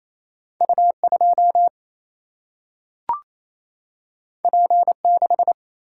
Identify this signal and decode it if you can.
Morse code, heard as a keyed tone, about 28 wpm, 700 Hz. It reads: U2 P6
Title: Single-Letter Single-Number 28wpm